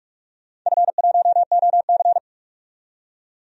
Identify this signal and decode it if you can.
Morse code, heard as a keyed tone, about 42 wpm, 700 Hz. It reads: F1OC
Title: DXCC-100 Call Signs - 4 Characters - 45wpm